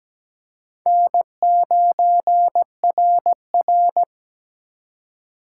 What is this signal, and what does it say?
Morse code, heard as a keyed tone, about 20 wpm, 700 Hz. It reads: N9RR
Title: Speed Racing - USA 1x2 Call Signs 17wpm